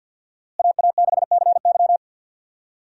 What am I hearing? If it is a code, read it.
Morse code, heard as a keyed tone, about 50 wpm, 700 Hz. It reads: AA6CY
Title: USA Call Signs - 5 Characters - Repeated 3x - 50wpm